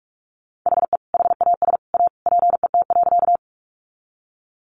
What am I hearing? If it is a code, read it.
Morse code, heard as a keyed tone, about 44 wpm, 700 Hz. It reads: HE HAS A PET.
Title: Easy Phrases 45wpm